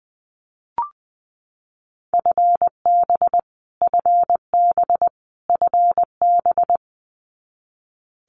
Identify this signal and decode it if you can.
Morse code, heard as a keyed tone, about 20 wpm, 700 Hz. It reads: FB FB FB
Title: Abbreviations Repeated 3x - 20wpm